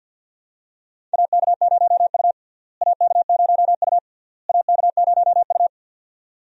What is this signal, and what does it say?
Morse code, heard as a keyed tone, about 52 wpm, 700 Hz. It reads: AK0U AK0U AK0U
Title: USA Call Signs - 4 Characters - Repeated 3x - 50wpm